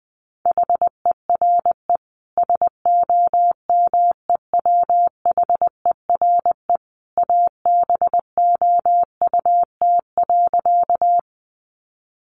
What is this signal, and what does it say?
Morse code, heard as a keyed tone, about 20 wpm, 700 Hz. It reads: HERE SOMEWHERE ABOUT.
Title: Sentences from Top 1400 Words 20wpm